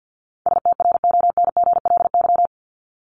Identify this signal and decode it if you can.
Morse code, heard as a keyed tone, about 50 wpm, 700 Hz. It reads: STRONGLY
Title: LY Suffix - Top 100 Words with Base Words 50wpm